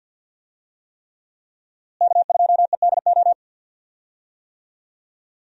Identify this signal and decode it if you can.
Morse code, heard as a keyed tone, about 48 wpm, 700 Hz. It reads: K2EDO
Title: USA Call Signs - 5 Characters - Repeated 3x - 50wpm